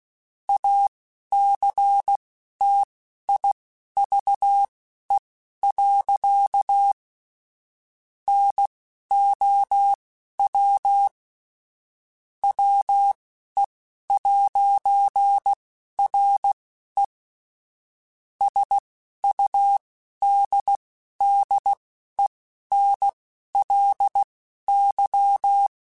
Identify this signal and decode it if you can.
Morse code, heard as a keyed tone, about 16 wpm, spaced as for 11 wpm, 800 Hz. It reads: ACTIVE. NOW WE'RE SUDDENLY
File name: Technical Subjects / Technical Subjects-mp3s / AND Gate @10